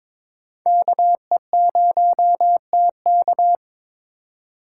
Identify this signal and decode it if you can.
Morse code, heard as a keyed tone, about 22 wpm, 700 Hz. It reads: KE0TK
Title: USA Call Signs - 4x Character Spacing - 22wpm